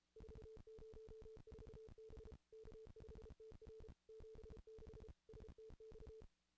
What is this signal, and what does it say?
Morse code, heard as a keyed tone, about 34 wpm, 440 Hz. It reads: 404 B K5TR 76 STX
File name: contests / ss-cw-k5tr-op-n5rz -0000